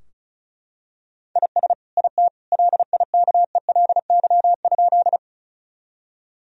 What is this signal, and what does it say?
Morse code, heard as a keyed tone, about 36 wpm, 700 Hz. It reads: IS IT LIKELY?
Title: Sentences from Top 1000 Words - Review Part 4 - 35wpm